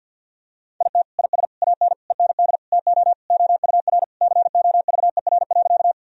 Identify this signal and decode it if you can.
Morse code, heard as a keyed tone, about 47 wpm, 700 Hz. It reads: IT IS AN END TO OUR COVER.
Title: Sentences from Top 1000 Words - Review Part 1 - 50wpm